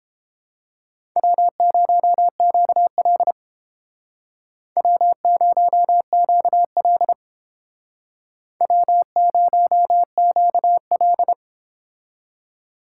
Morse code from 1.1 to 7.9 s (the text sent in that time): W0QL W0QL